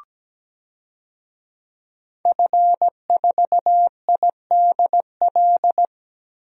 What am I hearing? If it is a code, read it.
Morse code, heard as a keyed tone, about 17 wpm, 700 Hz. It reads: F4IDL
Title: DXCC-100 Call Signs - 5 Characters - Repeated 2x - 17wpm